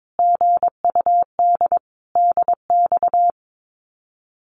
Morse code, heard as a keyed tone, about 22 wpm, 700 Hz. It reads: GUD DX
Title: Common QSO Phrases Encore 22wpm